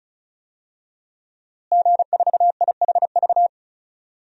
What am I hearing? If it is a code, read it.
Morse code, heard as a keyed tone, about 36 wpm, 700 Hz. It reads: G4IHV
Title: DXCC-100 Call Signs - 5 Characters - Repeated 2x - 35wpm